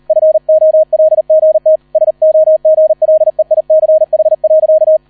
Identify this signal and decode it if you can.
Morse code, heard as a keyed tone, about 38 wpm, 630 Hz, with radio noise. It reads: FOLGT SOGLEICH.